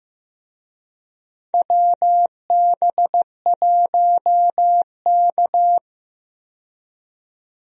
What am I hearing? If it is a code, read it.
Morse code, heard as a keyed tone, about 15 wpm, 700 Hz. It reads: WB1K
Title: USA Call Signs - 4 Characters - 15wpm